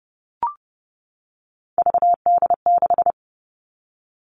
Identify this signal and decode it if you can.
Morse code, heard as a keyed tone, about 30 wpm, 700 Hz. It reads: VD6
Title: Three Challenging Letter-Number Combinations 30wpm